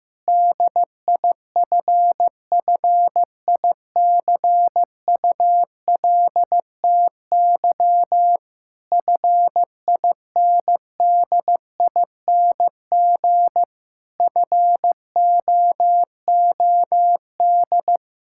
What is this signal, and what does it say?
Morse code, heard as a keyed tone, about 15 wpm, 700 Hz. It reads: DIFFICULTY FINDING FOOD
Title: Sentences from Top 1600 Words 15wpm